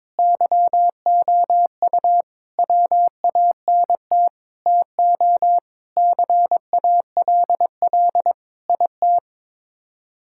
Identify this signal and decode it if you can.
Morse code, heard as a keyed tone, about 21 wpm, 700 Hz. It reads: YOU WANT TO CALL IT
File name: practice / 22wpm / Sets of 7 Words - 22wpm